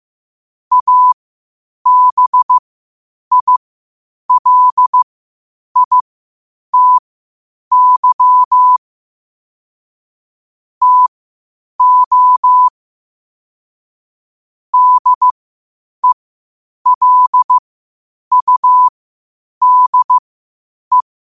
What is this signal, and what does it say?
Morse code, heard as a keyed tone, about 15 wpm, spaced as for 8 wpm, 1000 Hz. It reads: ABILITY TO DELUDE
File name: 8wpm